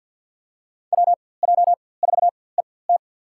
Morse code, heard as a keyed tone, about 53 wpm, spaced as for 24 wpm, 700 Hz. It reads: WJ4ET
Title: USA Call Signs - 5 Characters - 4x Character Spacing - 50wpm